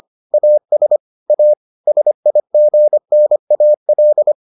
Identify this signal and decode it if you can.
Morse code, heard as a keyed tone, about 25 wpm, 600 Hz. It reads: AS A SIGNAL